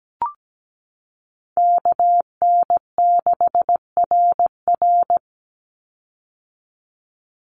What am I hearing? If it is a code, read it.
Morse code, heard as a keyed tone, about 17 wpm, 700 Hz. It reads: KN6RR
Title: USA Call Signs 17wpm